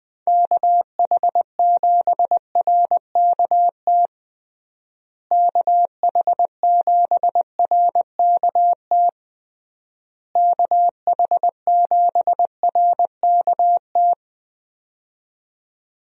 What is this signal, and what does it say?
Morse code, heard as a keyed tone, about 21 wpm, 700 Hz. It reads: KH7RKT KH7RKT KH7RKT
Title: USA Call Signs - 6 Characters - Repeated 3x - 20wpm